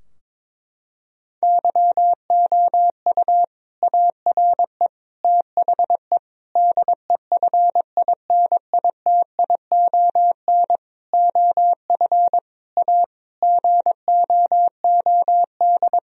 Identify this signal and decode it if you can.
Morse code, heard as a keyed tone, about 21 wpm, 700 Hz. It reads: YOU ARE THE DEFINITION OF A GOOD
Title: Sentences from Top 1000 Words - Review Part 4 - 22wpm